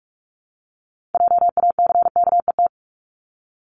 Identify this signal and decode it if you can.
Morse code, heard as a keyed tone, about 46 wpm, 700 Hz. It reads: JACKET
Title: Multi-letter Phonemes Repeated 3x 45wpm